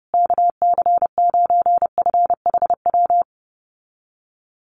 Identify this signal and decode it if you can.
Morse code, heard as a keyed tone, about 30 wpm, 700 Hz. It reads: KC9FHW
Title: USA Call Signs Repeated 3x 30wpm